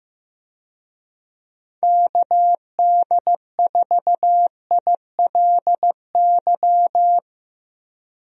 Morse code, heard as a keyed tone, about 15 wpm, 700 Hz. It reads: KD4ILY
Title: USA Call Signs - 6 Characters - 15wpm